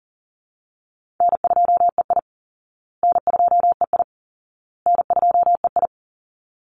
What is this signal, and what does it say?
Morse code, heard as a keyed tone, about 40 wpm, 700 Hz. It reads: N2EI N2EI N2EI
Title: Easy Call Signs Repeated 3x - 40wpm